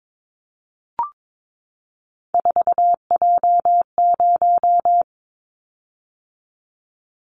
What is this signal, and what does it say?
Morse code, heard as a keyed tone, about 22 wpm, 700 Hz. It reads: 4J0
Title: Callsign Prefixes for Top 100 DXCC - 22wpm